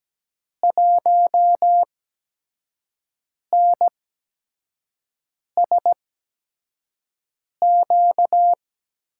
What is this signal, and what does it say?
Morse code, heard as a keyed tone, about 17 wpm, spaced as for 5 wpm, 700 Hz. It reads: 1NSQ
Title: USA Call Signs - 6 Characters - 8x Character Spacing - 17wpm